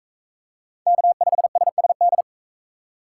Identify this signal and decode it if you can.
Morse code, heard as a keyed tone, about 43 wpm, 700 Hz. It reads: K5SSD